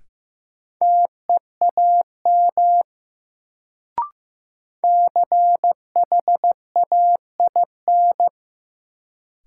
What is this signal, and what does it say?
Morse code, heard as a keyed tone, about 15 wpm, 700 Hz. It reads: TEAM CHAIN
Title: Sentences from Top 1500 Words - New Word Drill Alt - 15wpm